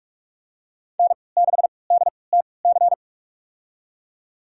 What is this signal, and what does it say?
Morse code, heard as a keyed tone, about 46 wpm, spaced as for 26 wpm, 700 Hz. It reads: N6DTC
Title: USA Call Signs - 3x Character Spacing - 45wpm